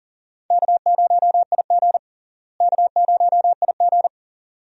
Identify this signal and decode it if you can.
Morse code, heard as a keyed tone, about 40 wpm, 700 Hz. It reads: K0IG K0IG
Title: USA Call Signs - 4 Characters - Repeated 2x - 40wpm